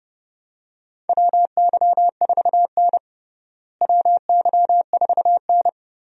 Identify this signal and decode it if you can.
Morse code, heard as a keyed tone, about 32 wpm, 700 Hz. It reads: WY4N WY4N
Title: USA Call Signs - 4 Characters - Repeated 2x - 30wpm